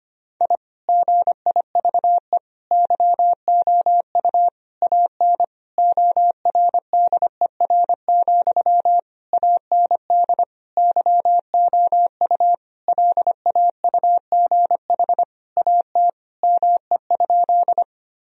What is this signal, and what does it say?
Morse code, heard as a keyed tone, about 25 wpm, 700 Hz. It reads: I GIVE YOU AN ORDER, AND YOU LAUGH AT ME?
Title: Sentences from Top 1000 Words - Review Part 2 - 25wpm